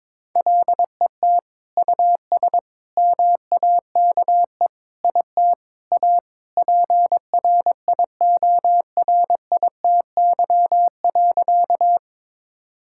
Morse code, heard as a keyed tone, about 22 wpm, 700 Hz. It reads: LET US MAKE IT A PRIORITY.
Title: Sentences from Top 1600 Words 22wpm